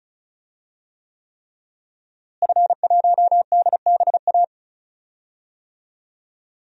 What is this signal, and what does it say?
Morse code, heard as a keyed tone, about 36 wpm, 700 Hz. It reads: F1DBA
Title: DXCC-100 Call Signs - 5 Characters - Repeated 3x - 35wpm